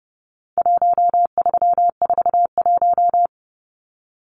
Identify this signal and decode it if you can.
Morse code, heard as a keyed tone, about 30 wpm, 700 Hz. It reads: 1341
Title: Numbers - 4-Digit 30wpm